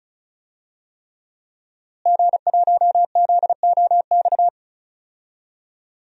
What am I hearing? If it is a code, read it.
Morse code, heard as a keyed tone, about 36 wpm, 700 Hz. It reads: G1ZOX
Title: DXCC-100 Call Signs - 5 Characters - Repeated 3x - 35wpm